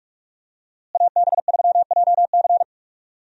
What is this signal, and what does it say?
Morse code, heard as a keyed tone, about 42 wpm, 700 Hz. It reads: AB3JC
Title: USA Call Signs - 5 Characters - Repeated 2x - 45wpm